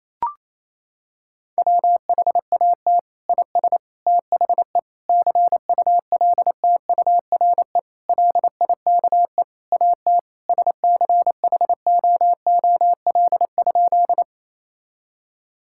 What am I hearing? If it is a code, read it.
Morse code, heard as a keyed tone, about 28 wpm, 700 Hz. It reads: WHAT IS THE CULTURE LIKE AT SCHOOL?
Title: Sentences from Top 1000 Words - Review Part 6 - 28wpm